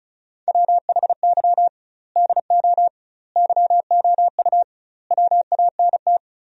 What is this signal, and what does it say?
Morse code, heard as a keyed tone, about 36 wpm, 700 Hz. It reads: WHY DO YOU WANT
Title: 2x Word-Spacing - Sentences from Top 200 Words - 35wpm